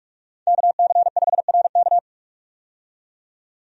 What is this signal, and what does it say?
Morse code, heard as a keyed tone, about 41 wpm, 700 Hz. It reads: KC5RK
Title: USA Call Signs - 5 Characters - 45wpm